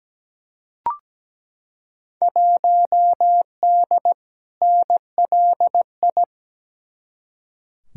Morse code, heard as a keyed tone, about 17 wpm, 700 Hz. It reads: 1D NLI
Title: Field Day Exchanges - 17wpm